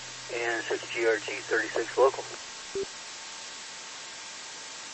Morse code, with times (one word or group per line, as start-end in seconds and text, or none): none